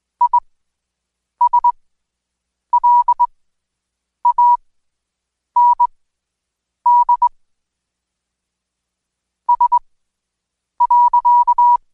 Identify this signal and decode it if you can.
Morse code, heard as a keyed tone, about 21 wpm, spaced as for 7 wpm, 970 Hz. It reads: ISLAND S.
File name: Code Quick / AudioDownld2 / Seg113-Day18-DayDrill 18